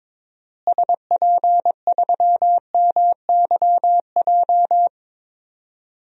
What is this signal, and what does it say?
Morse code, heard as a keyed tone, about 22 wpm, 700 Hz. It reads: SP3MYJ
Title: DXCC-100 Call Signs - 6 Characters - Repeated 3x - 22wpm